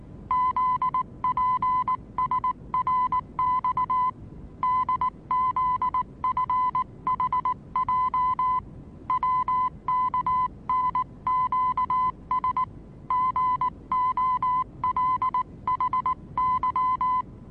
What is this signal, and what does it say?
Morse code, heard as a keyed tone, about 19 wpm, 1010 Hz. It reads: ZPSRX DZFHJ WKNQS GOLHY
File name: repos / Morsekurs / DJ4UF Morsen26